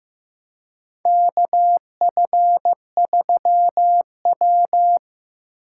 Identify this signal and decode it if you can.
Morse code, heard as a keyed tone, about 15 wpm, 700 Hz. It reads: KF3W
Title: USA Call Signs - 4 Characters - 2x Character Spacing - 15wpm